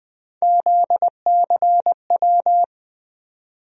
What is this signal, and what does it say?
Morse code, heard as a keyed tone, about 20 wpm, 700 Hz. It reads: ZCW
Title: USA Call Sign Postfixes - 3 Letters 20wpm